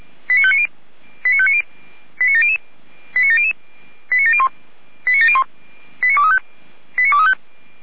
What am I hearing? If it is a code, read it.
Five-tone selective call (ZVEI), heard as quick runs of five tones, tones of about 70 ms. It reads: 89590 89590 89700 89700 89701 89701 89235 89235
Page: SDR/ZVEI